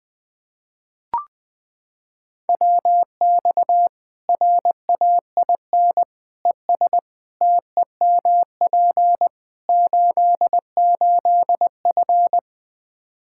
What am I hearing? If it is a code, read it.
Morse code, heard as a keyed tone, about 21 wpm, 700 Hz. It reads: WX RAIN ES TEMP 88F
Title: Practice QSO - Exchange 2 - Info - 20wpm